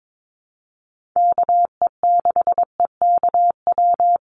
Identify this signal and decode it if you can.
Morse code, heard as a keyed tone, about 22 wpm, 700 Hz. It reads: KE6EKW
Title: USA Call Signs 22wpm